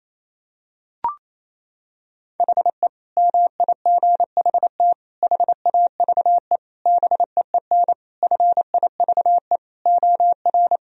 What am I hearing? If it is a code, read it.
Morse code, heard as a keyed tone, about 28 wpm, 700 Hz. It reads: HE MIGHT HAVE BEEN FIVE OR
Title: Sentences from Top 1000 Words - Review Part 2 - 28wpm